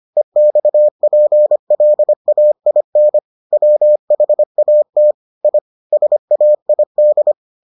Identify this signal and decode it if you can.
Morse code, heard as a keyed tone, about 25 wpm, 600 Hz. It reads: EXPLAIN WHAT I SAID